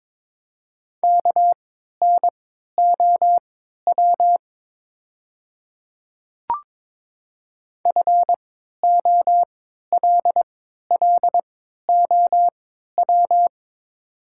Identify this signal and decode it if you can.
Morse code, heard as a keyed tone, about 22 wpm, spaced as for 13 wpm, 700 Hz. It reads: KNOW FOLLOW
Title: 3x Character Spacing - Top 300 Words - 22wpm